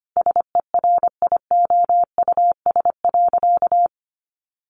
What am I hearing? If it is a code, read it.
Morse code, heard as a keyed tone, about 25 wpm, 700 Hz. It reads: SERIOUS.
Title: Sentences from Top 1100 Words 25wpm